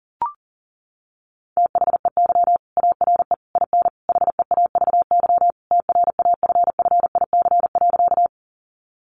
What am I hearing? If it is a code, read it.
Morse code, heard as a keyed tone, about 40 wpm, 700 Hz. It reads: THEY ARE IN HEAVY TRAFFIC.
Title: Sentences from Top 1300 Words 40wpm